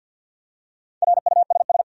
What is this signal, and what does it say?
Morse code, heard as a keyed tone, about 52 wpm, 700 Hz. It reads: RUSS
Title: USA Common Names - Max 5 Letters - 50wpm